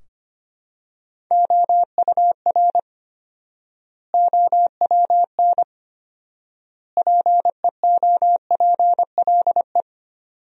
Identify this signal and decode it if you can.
Morse code, heard as a keyed tone, about 25 wpm, 700 Hz. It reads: OUR OWN PEOPLE
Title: 4x Word-Spacing - Sentences from Top 300 Words - 25wpm